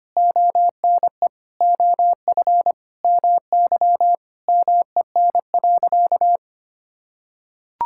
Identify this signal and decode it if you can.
Morse code, heard as a keyed tone, about 25 wpm, 700 Hz. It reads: ONE OF MY MEN.
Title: Sentences from Top 1000 Words - Review Part 6 - 25wpm